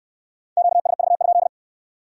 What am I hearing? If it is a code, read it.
Morse code, heard as a keyed tone, about 67 wpm, 700 Hz. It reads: QSV?